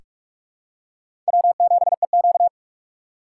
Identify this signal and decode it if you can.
Morse code, heard as a keyed tone, about 44 wpm, 690 Hz. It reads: W7EQ
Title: USA Call Signs - 4x Character Spacing - 45wpm